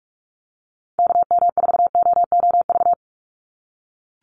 Morse code, heard as a keyed tone, about 44 wpm, 700 Hz. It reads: KM4OOV
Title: USA Call Signs 45wpm